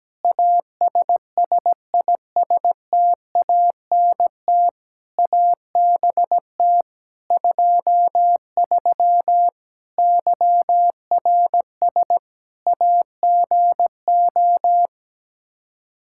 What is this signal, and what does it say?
Morse code, heard as a keyed tone, about 17 wpm, 700 Hz. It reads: ASSISTANT ABT 23 YRS AGO
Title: Practice QSO - Exchange 3 - More Info - 17wpm